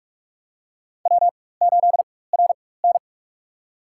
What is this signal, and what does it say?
Morse code, heard as a keyed tone, about 44 wpm, spaced as for 21 wpm, 700 Hz. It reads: W8RN
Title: USA Call Signs - 4 Characters - 4x Character Spacing - 45wpm